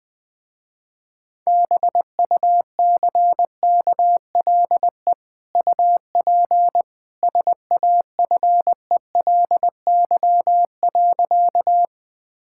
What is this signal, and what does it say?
Morse code, heard as a keyed tone, about 21 wpm, 700 Hz. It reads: BUCKLE UP SAFELY.
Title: Easy Sentences with Hints - 20wpm